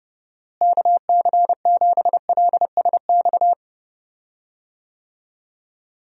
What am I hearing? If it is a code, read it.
Morse code, heard as a keyed tone, about 32 wpm, 700 Hz. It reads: KC7LSX
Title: USA Call Signs - 6 Characters - Repeated 3x - 30wpm